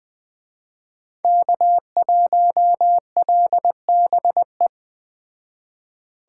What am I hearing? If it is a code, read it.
Morse code, heard as a keyed tone, about 20 wpm, 700 Hz. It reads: K1LBE